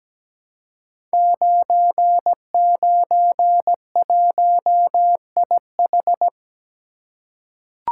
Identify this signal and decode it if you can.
Morse code, heard as a keyed tone, about 17 wpm, 700 Hz. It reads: 991IH
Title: Code Groups - 5 Letter:Numbers - 17wpm